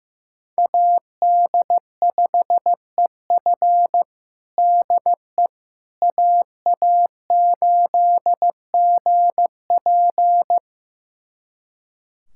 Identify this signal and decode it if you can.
Morse code, heard as a keyed tone, about 15 wpm, 700 Hz. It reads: AD5EF DE AA8GP
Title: USA Callsign DE Callsign - 5 Characters - 15wpm